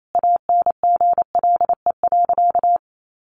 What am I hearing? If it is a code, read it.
Morse code, heard as a keyed tone, about 28 wpm, 700 Hz. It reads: ANGLE.
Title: Sentences from Top 1500 Words 28wpm